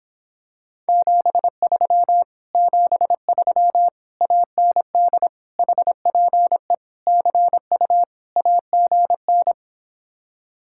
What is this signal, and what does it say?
Morse code, heard as a keyed tone, about 26 wpm, 700 Hz. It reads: 73 73 AND HPE CU AGN